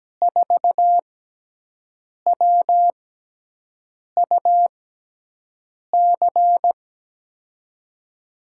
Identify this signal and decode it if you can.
Morse code, heard as a keyed tone, about 17 wpm, spaced as for 6 wpm, 700 Hz. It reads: 4WUC
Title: USA Call Signs - 6 Characters - 6x Character Spacing - 17wpm